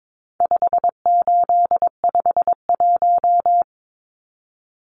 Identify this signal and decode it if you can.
Morse code, heard as a keyed tone, about 22 wpm, 700 Hz. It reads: 5851
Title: Numbers - 4-Digit 22wpm